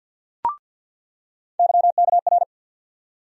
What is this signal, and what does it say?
Morse code, heard as a keyed tone, about 48 wpm, 700 Hz. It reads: YKR
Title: USA Call Sign Postfixes - 3 Letters 50wpm